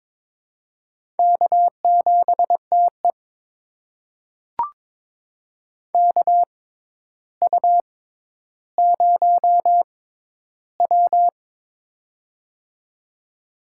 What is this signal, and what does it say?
Morse code, heard as a keyed tone, about 22 wpm, spaced as for 11 wpm, 700 Hz. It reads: K7TE KU0W
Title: USA Call Signs - 4 Characters - 6x Character Spacing - 22wpm